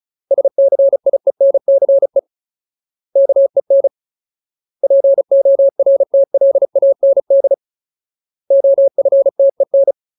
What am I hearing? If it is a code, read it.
Morse code, heard as a keyed tone, about 34 wpm, 550 Hz. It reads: SCIENCE KEN PORTLAND OFTEN
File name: Ultimate Word List / 35WPM Ultimate